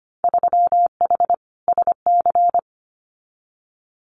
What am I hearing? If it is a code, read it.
Morse code, heard as a keyed tone, about 25 wpm, 700 Hz. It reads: 3H SC
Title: Winter Field Day Exchanges 25wpm